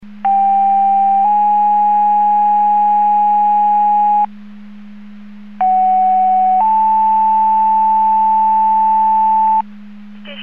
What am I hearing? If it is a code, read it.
Two-tone page, heard as a one-second tone, then a three-second tone, about 790 then 830 Hz, about 750 then 880 Hz. It is heard from 0.2 to 9.6 s.
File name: Portage 154.385 Paging